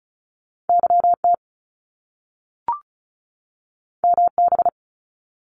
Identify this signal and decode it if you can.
Morse code, heard as a keyed tone, about 36 wpm, 700 Hz. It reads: YT MB
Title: Canadian Provinces and Territories 35wpm